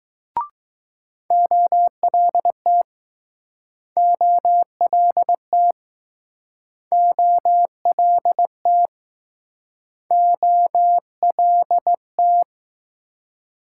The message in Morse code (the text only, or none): OLT OLT OLT OLT